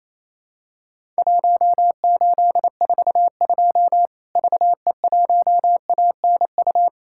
Morse code, heard as a keyed tone, about 28 wpm, 700 Hz. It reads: 1842 VE1ANU
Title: Worked All Europe DX Contest - QTC Exchange - 28wpm